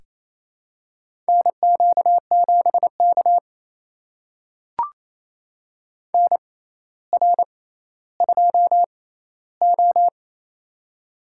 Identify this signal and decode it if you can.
Morse code, heard as a keyed tone, about 28 wpm, spaced as for 14 wpm, 700 Hz. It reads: NQ7K NR2O
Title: USA Call Signs - 4 Characters - 6x Character Spacing - 28wpm